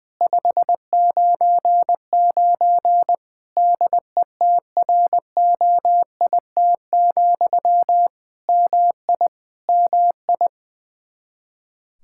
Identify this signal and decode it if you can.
Morse code, heard as a keyed tone, about 20 wpm, 700 Hz. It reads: 599 DETROIT, MI MI
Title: Practice QSO - Exchange 1 - Beginning - 20wpm